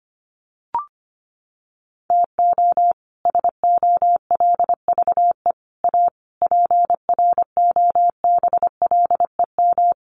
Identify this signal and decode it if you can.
Morse code, heard as a keyed tone, about 25 wpm, 700 Hz. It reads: TO SOLVE A PROBLEM
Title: Sentences from Top 1400 Words 25wpm